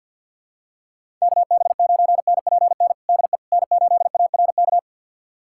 Morse code, heard as a keyed tone, about 46 wpm, 700 Hz. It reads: KB9NPN DE N8ARK
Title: USA Callsign DE Callsign 50wpm